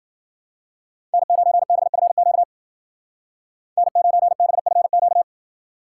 Morse code, heard as a keyed tone, about 50 wpm, 700 Hz. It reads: N9BFQ N9BFQ
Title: Speed Racing - USA 1x3 Call Signs 40wpm